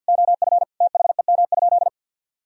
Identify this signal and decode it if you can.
Morse code, heard as a keyed tone, about 49 wpm, 700 Hz. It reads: OF THEM?
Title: Sentences from Top 1000 Words - Review Part 2 - 50wpm